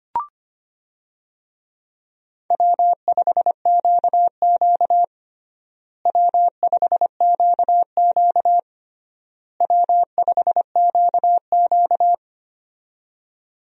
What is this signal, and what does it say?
Morse code, heard as a keyed tone, about 25 wpm, 700 Hz. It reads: W5QQ W5QQ W5QQ
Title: USA Call Signs - 4 Characters - Repeated 3x - 25wpm